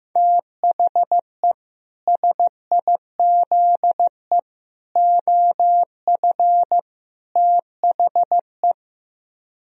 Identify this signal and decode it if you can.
Morse code, heard as a keyed tone, about 15 wpm, 700 Hz. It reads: THE SIZE OF THE
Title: Speed-Racing - Sets of 4 Words 15wpm